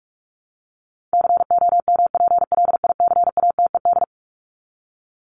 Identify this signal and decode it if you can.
Morse code, heard as a keyed tone, about 44 wpm, 700 Hz. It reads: COMPLICATED
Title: Top 1001-1250 Words 45wpm